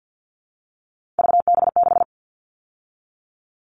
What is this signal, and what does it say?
Morse code, heard as a keyed tone, about 50 wpm, 700 Hz. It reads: VB6
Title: All Cut Numbers - 3-Digit 50wpm